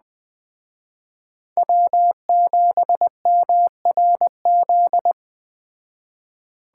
Morse code, heard as a keyed tone, about 20 wpm, 700 Hz. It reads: W7MRZ
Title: Speed Racing - USA 1x3 Call Signs 17wpm